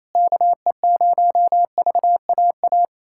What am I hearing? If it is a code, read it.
Morse code, heard as a keyed tone, about 28 wpm, 700 Hz. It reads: KE0VAA
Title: USA Callsign DE Callsign - 6 Characters - 28wpm